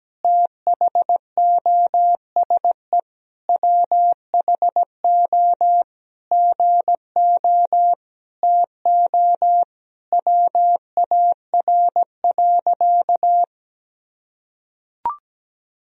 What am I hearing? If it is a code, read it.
Morse code, heard as a keyed tone, about 17 wpm, 700 Hz. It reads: THOSE WHO GO TO WAR.
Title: Sentences from Top 1000 Words - Review Part 2 - 17wpm